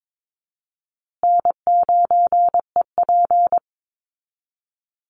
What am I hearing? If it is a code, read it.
Morse code, heard as a keyed tone, about 22 wpm, 700 Hz. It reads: N9EP
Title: Easy Call Signs Repeated 3x - 22wpm